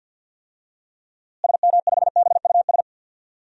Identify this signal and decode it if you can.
Morse code, heard as a keyed tone, about 46 wpm, 700 Hz. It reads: SM5BUS
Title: DXCC-100 Call Signs - 6 Characters - Repeated 2x - 50wpm